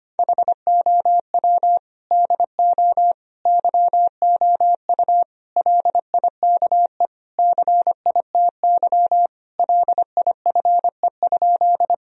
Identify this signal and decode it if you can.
Morse code, heard as a keyed tone, about 25 wpm, 700 Hz. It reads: HOW DO YOU LIKE CITY LIFE?
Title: Sentences from Top 1000 Words - Review Part 1 - 25wpm